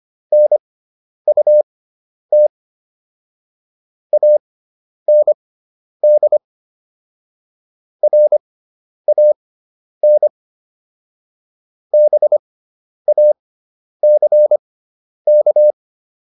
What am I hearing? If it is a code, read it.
Morse code, heard as a keyed tone, about 25 wpm, spaced as for 10 wpm, 600 Hz. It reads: NUT AND RAN BACK